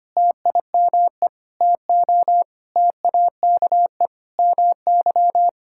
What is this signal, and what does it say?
Morse code, heard as a keyed tone, about 25 wpm, 700 Hz. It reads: TIME TO TAKE MY
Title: Sentences from Top 1000 Words - Review Part 5 - 25wpm